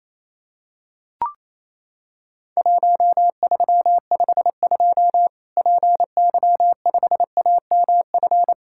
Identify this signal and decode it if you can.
Morse code, heard as a keyed tone, about 28 wpm, 700 Hz. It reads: 1352 PY5AMF
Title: Worked All Europe DX Contest - QTC Exchange - 28wpm